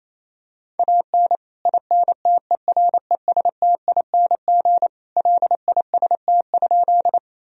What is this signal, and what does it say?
Morse code, heard as a keyed tone, about 28 wpm, 700 Hz. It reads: AN INTERESTING LIST?
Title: Sentences from Top 1000 Words - Review Part 2 - 28wpm